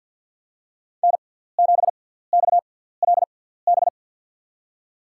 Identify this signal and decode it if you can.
Morse code, heard as a keyed tone, about 53 wpm, spaced as for 17 wpm, 690 Hz. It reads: N7XLB
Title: USA Call Signs - 5 Characters - 6x Character Spacing - 50wpm